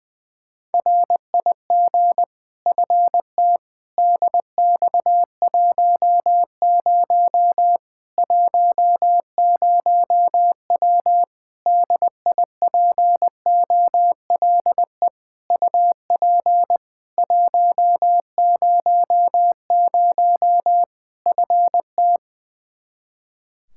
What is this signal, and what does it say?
Morse code, heard as a keyed tone, about 20 wpm, 700 Hz. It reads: RIG FT DX10 10W DIPOLE UP 100 FT
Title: Practice QSO - Exchange 2 - Info - 20wpm